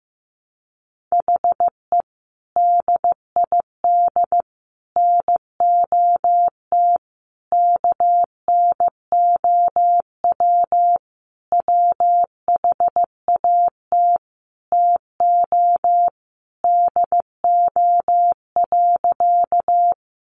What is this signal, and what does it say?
Morse code, heard as a keyed tone, about 14 wpm, 700 Hz. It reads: HE DID NOT KNOW WHAT TO DO.
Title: Sentences from Top 600 Words 15wpm